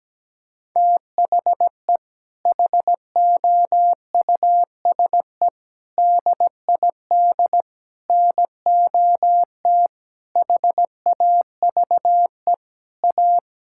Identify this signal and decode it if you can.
Morse code, heard as a keyed tone, about 17 wpm, 700 Hz. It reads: THE HOUSE DID NOT HAVE A
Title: Sentences from Top 1600 Words 17wpm